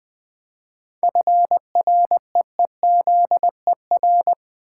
Morse code, heard as a keyed tone, about 20 wpm, 700 Hz. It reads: FREEZER
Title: Easy Words with Hints - 20wpm